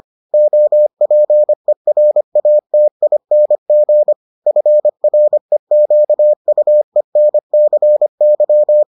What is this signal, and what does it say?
Morse code, heard as a keyed tone, about 25 wpm, 600 Hz. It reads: OPERATING FREQUENCY